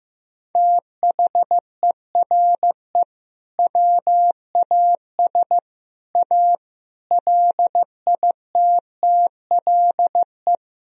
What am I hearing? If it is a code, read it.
Morse code, heard as a keyed tone, about 16 wpm, 700 Hz. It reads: THERE WAS A LITTLE
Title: Sentences from Top 1000 Words - Review Part 4 - 15wpm